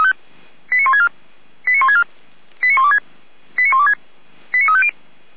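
Five-tone selective call (ZVEI), heard as quick runs of five tones, tones of about 70 ms. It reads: 89164 89164 89126 89126 89350
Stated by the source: ZVEI five-tone selective call received by SDR.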